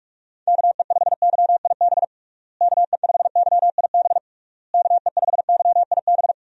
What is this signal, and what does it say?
Morse code, heard as a keyed tone, about 46 wpm, 700 Hz. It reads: KE5YIB KE5YIB KE5YIB
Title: USA Call Signs - 6 Characters - Repeated 3x - 45wpm